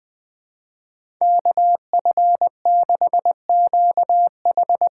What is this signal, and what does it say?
Morse code, heard as a keyed tone, about 20 wpm, 700 Hz. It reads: KF6QH
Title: USA Call Signs - 6x Character Spacing - 20wpm